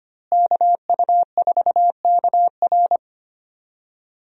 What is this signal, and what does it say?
Morse code, heard as a keyed tone, about 25 wpm, 700 Hz. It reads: KU4KR
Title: USA Call Signs - 5 Characters - Repeated 3x - 25wpm